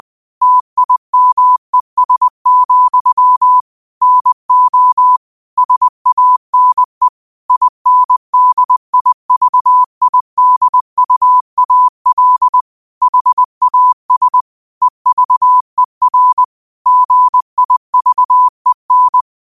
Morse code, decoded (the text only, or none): TIMES, NO SANE INDIVIDUAL HAS EVER GIVEN